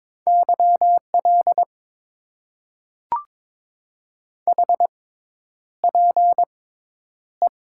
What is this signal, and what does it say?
Morse code, heard as a keyed tone, about 21 wpm, spaced as for 10 wpm, 700 Hz. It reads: YL HPE
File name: Top 90 QSO Elements - 6x Character Spacing - 22wpm